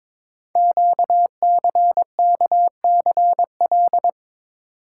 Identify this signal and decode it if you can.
Morse code, heard as a keyed tone, about 23 wpm, 700 Hz. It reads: QCKCL
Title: Code Groups - 5 Letters - 22wpm